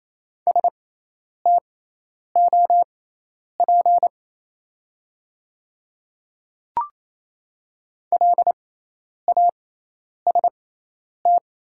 Morse code, heard as a keyed tone, about 28 wpm, spaced as for 10 wpm, 700 Hz. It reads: STOP LAST
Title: 6x Character Spacing - Top 300 Words - 28wpm